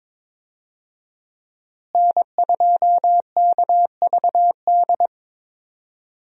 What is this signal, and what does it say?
Morse code, heard as a keyed tone, about 22 wpm, 700 Hz. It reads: N2KVD
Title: USA Call Signs - 5 Characters - Repeated 2x - 22wpm